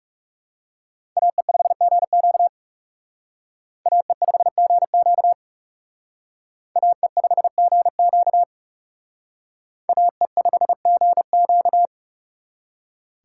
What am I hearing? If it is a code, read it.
Morse code, heard as a keyed tone, about 36 wpm, 700 Hz. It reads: AE5GQ AE5GQ AE5GQ AE5GQ
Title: Speed Racing - USA 2x2 Call Signs 30wpm